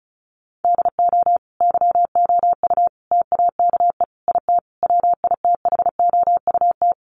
Morse code, heard as a keyed tone, about 34 wpm, 700 Hz. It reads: DO YOU TAKE IT WITHOUT
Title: Sentences from Top 1200 Words 35wpm